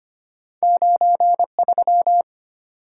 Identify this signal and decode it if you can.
Morse code, heard as a keyed tone, about 25 wpm, 700 Hz. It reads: 93
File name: Two Letter-Number Combinations - 25wpm